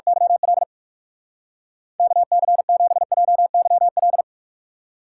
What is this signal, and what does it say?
Morse code, heard as a keyed tone, about 46 wpm, 700 Hz. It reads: YL KC7JYL
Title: Speed Racing - USA Call Signs 45wpm